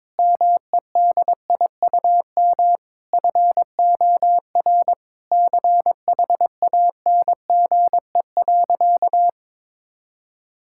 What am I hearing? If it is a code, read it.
Morse code, heard as a keyed tone, about 22 wpm, 700 Hz. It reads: MEDIUM FOR CHANGE.
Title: Sentences from Top 1000 Words - Review Part 6 - 22wpm